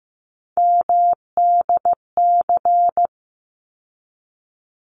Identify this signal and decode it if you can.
Morse code, heard as a keyed tone, about 15 wpm, 700 Hz. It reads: MDC
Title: ARRL Sections 15wpm